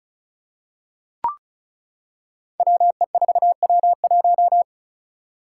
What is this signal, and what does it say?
Morse code, heard as a keyed tone, about 36 wpm, 700 Hz. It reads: WE4W1
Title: Five Challenging Letter-Number Combinations 35wpm